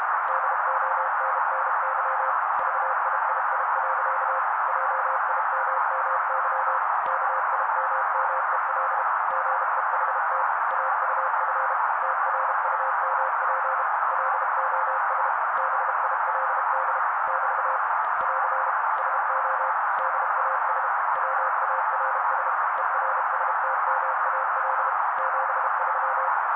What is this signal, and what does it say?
Morse code, heard as a keyed tone, about 32 wpm, 530 Hz. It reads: DONNY VIII. JIMMY KIMMEL GEHT AUF TRUMP LOS BILD <BT> <AR> SO DAS WARS ERSTMAL 73
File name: daily / 31wpm noise